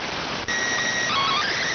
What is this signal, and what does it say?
SSTV (slow-scan television), PD 290.